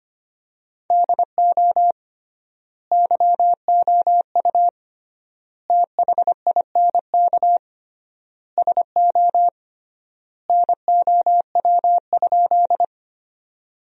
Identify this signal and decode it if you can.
Morse code, heard as a keyed tone, about 25 wpm, 700 Hz. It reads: DO YOU THINK SO NOW?
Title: 3x Word-Spacing - Sentences from Top 200 Words - 25wpm